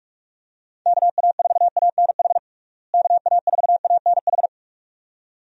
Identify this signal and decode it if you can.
Morse code, heard as a keyed tone, about 46 wpm, 700 Hz. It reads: KA4ANH KA4ANH
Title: USA Call Signs - 6 Characters - Repeated 2x - 45wpm